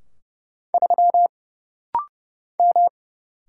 Morse code, heard as a keyed tone, about 30 wpm, 700 Hz. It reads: 3 M
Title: Single Letter-Number Alt - Rapid-Fire 30wpm